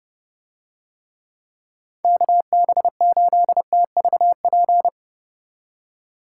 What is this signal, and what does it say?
Morse code, heard as a keyed tone, about 31 wpm, 700 Hz. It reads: KB8TVP